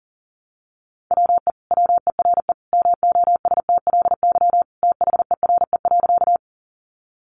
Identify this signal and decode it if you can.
Morse code, heard as a keyed tone, about 37 wpm, 700 Hz. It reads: WE WERE MOSTLY THERE.
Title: Sentences from Top 1300 Words 40wpm